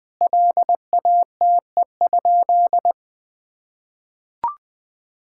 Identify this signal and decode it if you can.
Morse code, heard as a keyed tone, about 20 wpm, 700 Hz. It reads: LATE?
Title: Sentences from Top 1000 Words - Review Part 5 - 20wpm